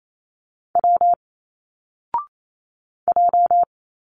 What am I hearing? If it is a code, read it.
Morse code, heard as a keyed tone, about 28 wpm, 700 Hz. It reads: W J
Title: Single Letters 28wpm